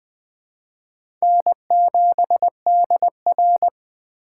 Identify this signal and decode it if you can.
Morse code, heard as a keyed tone, about 20 wpm, 700 Hz. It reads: N7DR
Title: K1USN SST QSO Practice - 20wpm